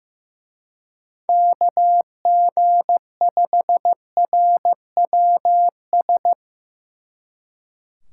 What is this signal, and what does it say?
Morse code, heard as a keyed tone, about 16 wpm, 700 Hz. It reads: KG5RWS
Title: Speed Racing - USA 2x3 Call Signs 15wpm